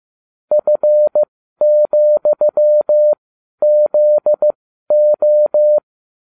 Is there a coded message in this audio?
Yes: Morse code from 0.5 to 5.8 s, heard as a keyed tone, about 15 wpm, spaced as for 12 wpm, 600 Hz. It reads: F,ZO